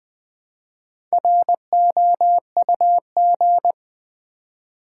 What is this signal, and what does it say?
Morse code, heard as a keyed tone, about 20 wpm, 700 Hz. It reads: ROUG
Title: Build A Word - 1 Letter At A Time - 5-Letter Words - 20wpm